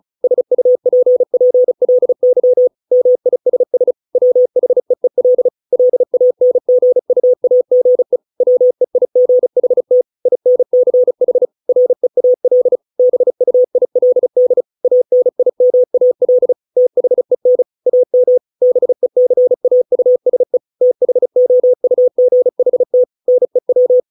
Morse code, heard as a keyed tone, about 34 wpm, 500 Hz. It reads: SUPPLY MISS WHEEL LANGUAGE WEIGHT INCH REAL BUILD ANIMAL THEN AM BECAUSE THOUGHT NEW